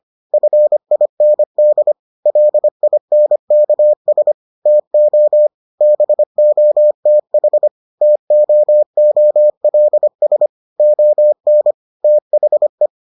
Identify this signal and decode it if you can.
Morse code, heard as a keyed tone, about 25 wpm, 600 Hz. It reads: FIND LINKS TO BOTH TOOLS ON THE